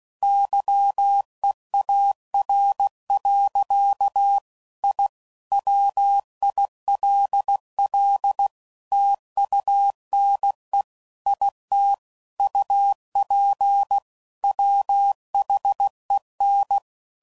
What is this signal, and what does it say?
Morse code, heard as a keyed tone, about 16 wpm, 800 Hz. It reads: YEAR. I WILL TUNE IT UP WHEN
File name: CW Code Groups @15WPM